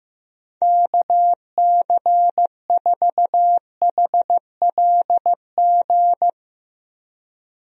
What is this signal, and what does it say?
Morse code, heard as a keyed tone, about 16 wpm, 700 Hz. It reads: KC4HLG
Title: USA Call Signs - 6 Characters - Repeated 3x - 15wpm